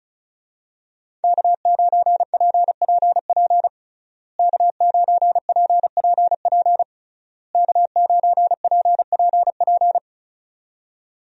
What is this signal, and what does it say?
Morse code, heard as a keyed tone, about 34 wpm, 700 Hz. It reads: K9PPP K9PPP K9PPP
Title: USA Call Signs - 5 Characters - Repeated 3x - 35wpm